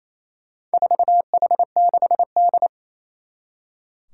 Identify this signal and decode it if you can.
Morse code, heard as a keyed tone, about 28 wpm, 700 Hz. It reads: 4H6D
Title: Four Challenging Letter-Number Combinations 28wpm